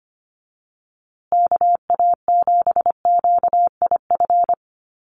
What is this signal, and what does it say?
Morse code, heard as a keyed tone, about 25 wpm, 700 Hz. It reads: KA7QIF
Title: USA Call Signs Repeated 2x 25wpm